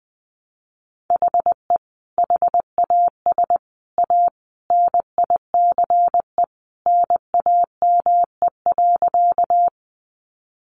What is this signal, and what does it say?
Morse code, heard as a keyed tone, about 20 wpm, 700 Hz. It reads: HE HAS A NICE NAME.
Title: Sentences from Top 1100 Words 20wpm